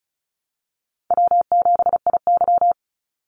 Morse code, heard as a keyed tone, about 34 wpm, 700 Hz. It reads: W7IY
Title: Easy Call Signs Repeated 3x - 35wpm